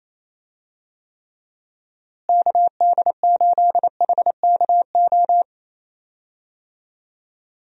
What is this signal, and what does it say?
Morse code, heard as a keyed tone, about 28 wpm, 700 Hz. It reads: KD8HKO
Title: USA Call Signs - 6 Characters - Repeated 3x - 28wpm